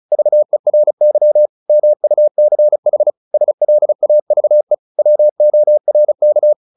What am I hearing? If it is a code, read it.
Morse code, heard as a keyed tone, about 34 wpm, 600 Hz. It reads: VERY MUCH SLAVE WORK
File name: the old man and the sea 2 35wpm